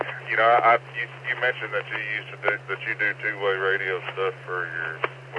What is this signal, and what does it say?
Emergency Alert System (SAME) tones, heard as harsh buzzing bursts, with speech. It is heard from 1.2 to 4.9 s.